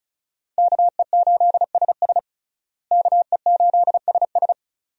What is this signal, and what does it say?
Morse code, heard as a keyed tone, about 36 wpm, 700 Hz. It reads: KE8SS KE8SS
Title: USA Call Signs - 5 Characters - Repeated 2x - 35wpm